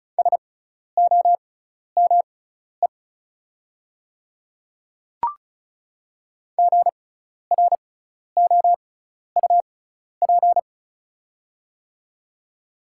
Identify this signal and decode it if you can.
Morse code, heard as a keyed tone, about 36 wpm, spaced as for 12 wpm, 700 Hz. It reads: SOME GROUP
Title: 6x Character Spacing - Top 500 Words - 35wpm